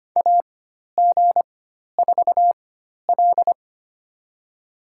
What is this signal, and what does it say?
Morse code, heard as a keyed tone, about 25 wpm, spaced as for 12 wpm, 700 Hz. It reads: AG4L
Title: USA Call Signs - 4x Character Spacing - 25wpm